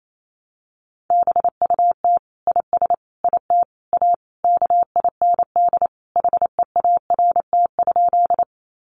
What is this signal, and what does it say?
Morse code, heard as a keyed tone, about 28 wpm, 700 Hz. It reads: BUT IS IT A KIND HEART?
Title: Sentences from Top 1200 Words 28wpm